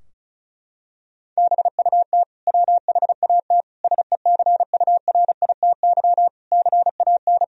Morse code, heard as a keyed tone, about 36 wpm, 700 Hz. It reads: BUT WHAT SECURITY CAN
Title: Sentences from Top 1000 Words - Review Part 4 - 35wpm